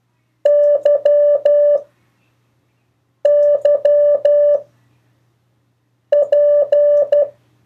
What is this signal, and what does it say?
Morse code, heard as a keyed tone, about 12 wpm, 580 Hz. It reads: Y Y P
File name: 1of4-cw-letters